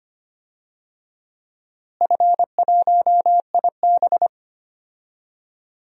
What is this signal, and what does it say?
Morse code, heard as a keyed tone, about 25 wpm, 700 Hz. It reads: F1IB
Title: DXCC-100 Call Signs - 4 Characters - Repeated 3x - 25wpm